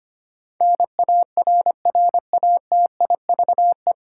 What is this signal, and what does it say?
Morse code, heard as a keyed tone, about 25 wpm, 700 Hz. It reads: NARRATIVE
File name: practice / 25wpm / 9 Letter Words - First 3 Letters Given - 25wpm